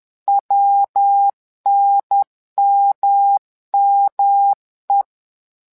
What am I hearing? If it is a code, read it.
Morse code, heard as a keyed tone, about 10 wpm, 800 Hz. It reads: WNMME
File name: LEZIONI / 3 MWRO / 3 MWRO 10wpm 2148